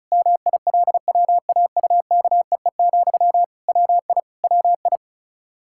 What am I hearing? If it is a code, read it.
Morse code, heard as a keyed tone, about 34 wpm, 700 Hz. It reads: MILWAUKEE, WI WI
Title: Practice QSO - Exchange 1 - Beginning - 35wpm